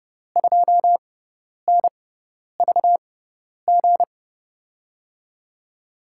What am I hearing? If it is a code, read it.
Morse code, heard as a keyed tone, about 30 wpm, spaced as for 10 wpm, 700 Hz. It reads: 2NVG
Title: USA Call Signs - 6x Character Spacing - 30wpm